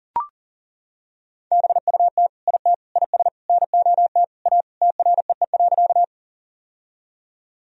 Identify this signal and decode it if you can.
Morse code, heard as a keyed tone, about 40 wpm, 700 Hz. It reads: BUT IT IS NOT A TREE.
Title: Sentences from Top 1000 Words - Review Part 1 - 40wpm